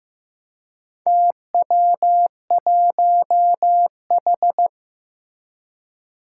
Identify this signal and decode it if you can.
Morse code, heard as a keyed tone, about 15 wpm, 700 Hz. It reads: TW1H
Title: Four Challenging Letter-Number Combinations 15wpm